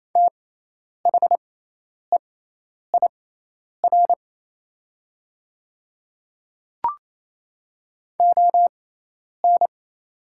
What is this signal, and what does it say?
Morse code, heard as a keyed tone, about 27 wpm, spaced as for 10 wpm, 700 Hz. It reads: THEIR ON
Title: 6x Character Spacing - Top 100 Words - 28wpm